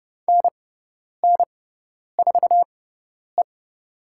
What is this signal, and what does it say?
Morse code, heard as a keyed tone, about 31 wpm, spaced as for 10 wpm, 700 Hz. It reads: NN4E
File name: Lesson 16g - Letter c - Call Signs 30wpm